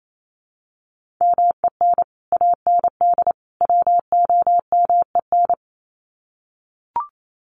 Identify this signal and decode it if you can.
Morse code, heard as a keyed tone, about 28 wpm, 700 Hz. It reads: MEN AND WOMEN
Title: Sets of 3 Words 28wpm